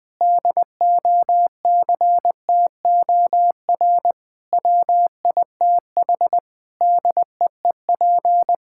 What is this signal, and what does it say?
Morse code, heard as a keyed tone, about 21 wpm, 700 Hz. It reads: DOCTOR WITH DEEP
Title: Sentences from Top 1600 Words 20wpm